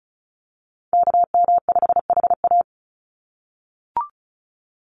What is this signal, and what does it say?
Morse code, heard as a keyed tone, about 34 wpm, 700 Hz. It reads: KM5HA
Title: USA Call Signs 35wpm